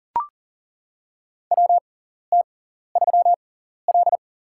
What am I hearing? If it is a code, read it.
Morse code, heard as a keyed tone, about 40 wpm, spaced as for 14 wpm, 700 Hz. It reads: WT3L
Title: USA Call Signs - 6x Character Spacing - 40wpm